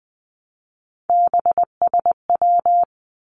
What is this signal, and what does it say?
Morse code, heard as a keyed tone, about 20 wpm, 700 Hz. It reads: BSW
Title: Three Challenging Letter-Number Combinations 20wpm